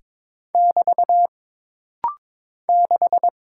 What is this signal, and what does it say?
Morse code, heard as a keyed tone, about 22 wpm, 700 Hz. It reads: <BT> 6
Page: This rapid-fire timing is geared to push your ICR (Instant Character Recognition) skills.